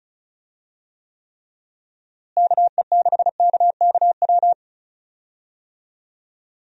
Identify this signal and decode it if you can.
Morse code, heard as a keyed tone, about 36 wpm, 700 Hz. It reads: KE6KKW
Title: USA Call Signs - 6 Characters - Repeated 3x - 35wpm